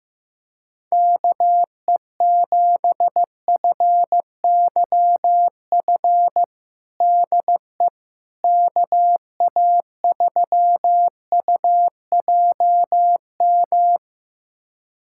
Morse code, heard as a keyed tone, about 15 wpm, 700 Hz. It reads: KE7FYF DE KA3UJM
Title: USA Callsign DE Callsign - 6 Characters - 15wpm